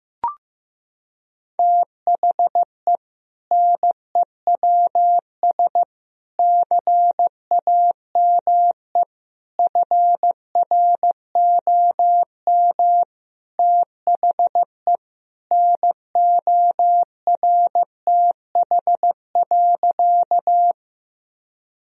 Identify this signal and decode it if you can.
Morse code, heard as a keyed tone, about 15 wpm, 700 Hz. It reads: THE NEWS CAME FROM THE NORTH.
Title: Sentences from Top 1000 Words - Review Part 1 - 15wpm